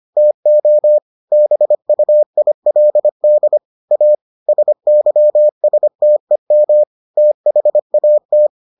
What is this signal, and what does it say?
Morse code, heard as a keyed tone, about 25 wpm, 600 Hz. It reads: TO BUILD A SYSTEM THAT